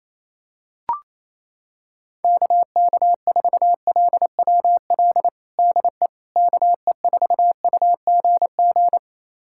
Morse code, heard as a keyed tone, about 28 wpm, 700 Hz. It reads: KK4LWL DE KE4UGG
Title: USA Callsign DE Callsign - 6 Characters - 28wpm